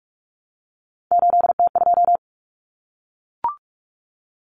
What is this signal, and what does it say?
Morse code, heard as a keyed tone, about 44 wpm, 700 Hz. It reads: 8T2
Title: Common Cut Numbers - 3-Digit 45wpm